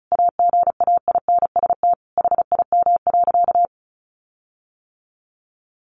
Morse code, heard as a keyed tone, about 34 wpm, 700 Hz. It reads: AGAINST HIM.
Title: Sentences from Top 1300 Words 35wpm